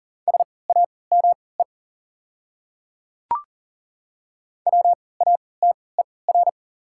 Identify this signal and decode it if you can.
Morse code, heard as a keyed tone, about 40 wpm, spaced as for 23 wpm, 690 Hz. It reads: SAME WATER
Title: 3x Character Spacing - Top 300 Words - 40wpm